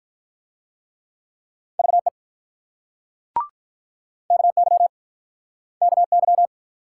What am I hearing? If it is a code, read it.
Morse code, heard as a keyed tone, about 50 wpm, 700 Hz. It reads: VE KY KY